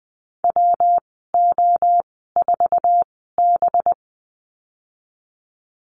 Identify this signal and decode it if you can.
Morse code, heard as a keyed tone, about 20 wpm, spaced as for 15 wpm, 700 Hz. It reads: WO4B
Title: 186 Lesson 30g - Letter b - Call Signs 20wpm